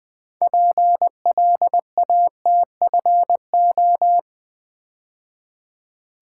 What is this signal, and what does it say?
Morse code, heard as a keyed tone, about 20 wpm, 700 Hz. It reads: PLATFO
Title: Build A Word - 3 Letters At A Time - 8-Letter Words - 20wpm